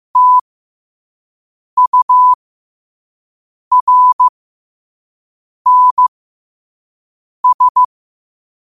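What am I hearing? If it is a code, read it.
Morse code, heard as a keyed tone, about 15 wpm, spaced as for 5 wpm, 1000 Hz. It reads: TURNS
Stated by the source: Morse code sent at 5 wpm.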